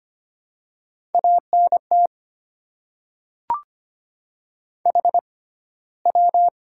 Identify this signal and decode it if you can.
Morse code, heard as a keyed tone, about 25 wpm, spaced as for 15 wpm, 700 Hz. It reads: ANT HW
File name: Top 20 QSO Elements - 6x Character Spacing - 25wpm